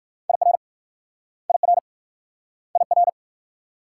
Morse code, heard as a keyed tone, about 56 wpm, 700 Hz. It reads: IR IR IR
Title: Speed-Racing Multi-letter Phonemes - 45wpm